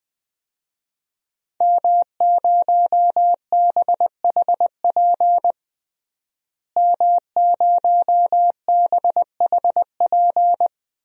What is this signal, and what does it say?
Morse code, heard as a keyed tone, about 20 wpm, 700 Hz. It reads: M0BHP M0BHP
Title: DXCC-100 Call Signs - 5 Characters - Repeated 3x - 20wpm